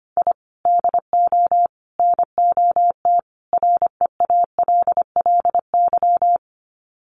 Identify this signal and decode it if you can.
Morse code, heard as a keyed tone, about 25 wpm, 700 Hz. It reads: I DO NOT REALLY
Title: Sets of 4 Words 25wpm